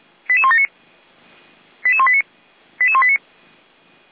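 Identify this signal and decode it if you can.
Five-tone selective call (ZVEI), heard as quick runs of five tones, tones of about 70 ms. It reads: 88179 88189 88189
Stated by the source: ZVEI five-tone selective call received on an SDR.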